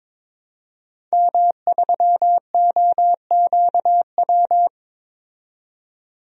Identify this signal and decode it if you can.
Morse code, heard as a keyed tone, about 22 wpm, 700 Hz. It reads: M3OQW
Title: DXCC-100 Call Signs - 5 Characters - 22wpm